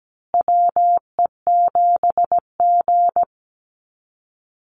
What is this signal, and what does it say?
Morse code, heard as a keyed tone, about 17 wpm, 700 Hz. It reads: WE7G
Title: Call Signs Repeated 3x - 17wpm